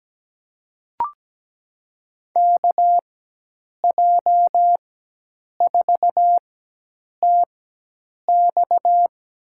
Morse code, heard as a keyed tone, about 17 wpm, spaced as for 8 wpm, 700 Hz. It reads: KJ4TX
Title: USA Call Signs - 4x Character Spacing - 17wpm